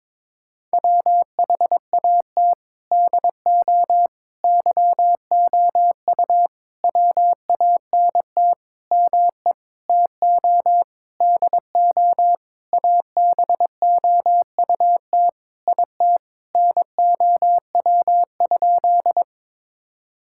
Morse code, heard as a keyed tone, about 22 wpm, 700 Hz. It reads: WHAT DO YOU WANT ME TO DO ABOUT IT NOW?
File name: Sentences with 7-Word Phrases - 22wpm